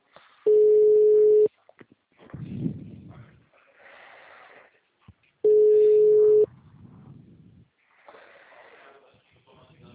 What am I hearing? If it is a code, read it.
Ringback tone (Europe, 425 Hz).